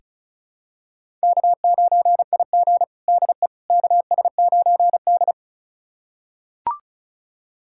Morse code, heard as a keyed tone, about 36 wpm, 700 Hz. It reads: K9IG DE KS9D
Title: USA Callsign DE Callsign - 4 Characters - 35wpm